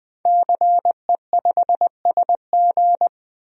Morse code, heard as a keyed tone, about 21 wpm, 700 Hz. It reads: CE5SG
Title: DXCC-100 Call Signs - 5 Characters - 20wpm